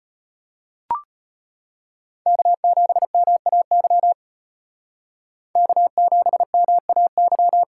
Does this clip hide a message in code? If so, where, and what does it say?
Yes: Morse code from 2.3 to 7.6 s, heard as a keyed tone, about 34 wpm, 700 Hz. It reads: K7MAY K7MAY